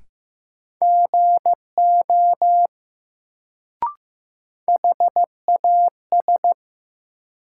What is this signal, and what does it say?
Morse code, heard as a keyed tone, about 15 wpm, 700 Hz. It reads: GO HAS
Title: Sentences from Top 100 Words - New Word Drill - 15wpm